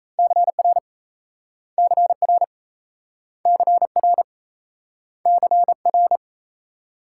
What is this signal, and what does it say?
Morse code, heard as a keyed tone, about 37 wpm, 700 Hz. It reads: CR CR CR CR